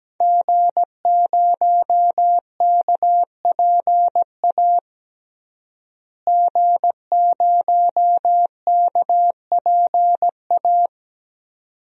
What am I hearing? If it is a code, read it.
Morse code, heard as a keyed tone, about 17 wpm, 700 Hz. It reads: G0KPA G0KPA
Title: DXCC-100 Call Signs - 5 Characters - Repeated 2x - 17wpm